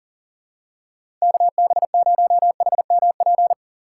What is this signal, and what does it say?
Morse code, heard as a keyed tone, about 40 wpm, 700 Hz. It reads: KB0HMP
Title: USA Call Signs - 6 Characters - 3x Character Spacing - 40wpm